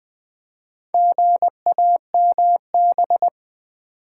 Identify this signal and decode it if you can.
Morse code, heard as a keyed tone, about 20 wpm, 700 Hz. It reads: GAMB
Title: Build A Word - 2 Letters At A Time - 8-Letter Words - 20wpm